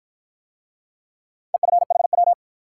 Morse code, heard as a keyed tone, about 48 wpm, 700 Hz. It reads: EFHW